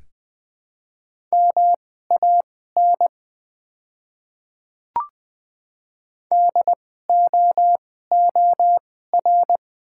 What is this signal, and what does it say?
Morse code, heard as a keyed tone, about 20 wpm, spaced as for 15 wpm, 690 Hz. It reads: MAN DOOR
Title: 2x Character Spacing - Top 500 Words - 20wpm